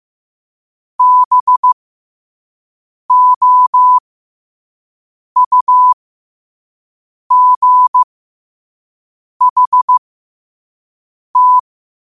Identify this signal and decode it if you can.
Morse code, heard as a keyed tone, about 15 wpm, spaced as for 5 wpm, 1000 Hz. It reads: BOUGHT
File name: qotdfiles / 5wpm